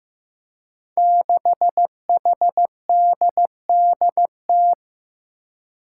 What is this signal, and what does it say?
Morse code, heard as a keyed tone, about 15 wpm, 700 Hz. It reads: 6HDDT
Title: Five Challenging Letter-Number Combinations 15wpm